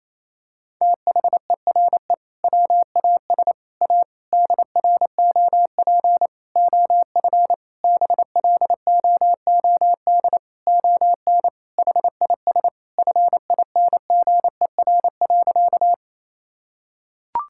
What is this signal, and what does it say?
Morse code, heard as a keyed tone, about 28 wpm, 700 Hz. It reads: THERE WAS A DROP OF BLOOD ON HIS FINGER.
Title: Sentences from Top 1600 Words 28wpm